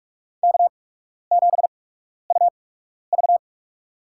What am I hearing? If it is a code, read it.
Morse code, heard as a keyed tone, about 46 wpm, spaced as for 12 wpm, 700 Hz. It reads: K7UV
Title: USA Call Signs - 4 Characters - 8x Character Spacing - 45wpm